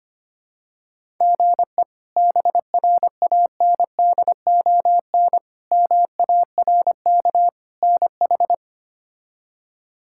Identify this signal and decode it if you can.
Morse code, heard as a keyed tone, about 25 wpm, 700 Hz. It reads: GE BRANDON MARK NH
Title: K1USN SST QSO Practice - 25wpm